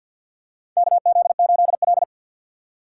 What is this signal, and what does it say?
Morse code, heard as a keyed tone, about 53 wpm, 700 Hz. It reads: KZ8L
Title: USA Call Signs - 4 Characters - 6x Character Spacing - 50wpm